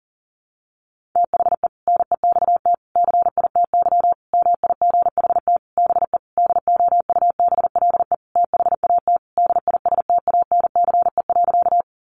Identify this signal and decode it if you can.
Morse code, heard as a keyed tone, about 40 wpm, 700 Hz. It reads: THE NEXT CITY MIGHT BE DOUBLE THAT DISTANCE.
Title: Sentences from Top 1400 Words 40wpm